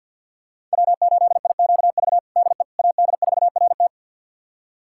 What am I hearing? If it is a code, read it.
Morse code, heard as a keyed tone, about 51 wpm, 700 Hz. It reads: W8IQV DE AD4RT
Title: USA Callsign DE Callsign 50wpm